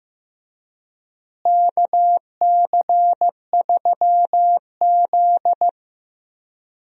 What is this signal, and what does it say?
Morse code, heard as a keyed tone, about 15 wpm, 700 Hz. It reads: KC3Z
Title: USA Call Signs - 4 Characters - Repeated 3x - 15wpm